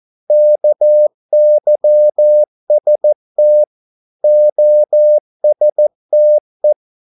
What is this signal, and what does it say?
Morse code, heard as a keyed tone, about 14 wpm, 600 Hz. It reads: KYST OSTE
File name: Korte ord DK 14wpm